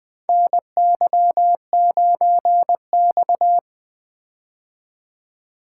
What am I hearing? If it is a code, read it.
Morse code, heard as a keyed tone, about 20 wpm, 700 Hz. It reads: NY9X
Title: K1USN SST QSO Practice - 20wpm